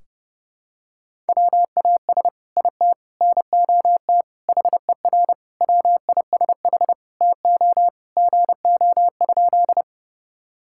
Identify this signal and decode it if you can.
Morse code, heard as a keyed tone, about 32 wpm, 700 Hz. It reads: WAS IT NOT HER WISH TO GO?
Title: Sentences from Top 1600 Words 30wpm